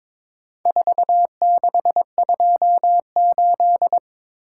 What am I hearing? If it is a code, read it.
Morse code, heard as a keyed tone, about 23 wpm, 700 Hz. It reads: 4628
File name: practice / 22wpm / Four Letter-Number Combinations - 22wpm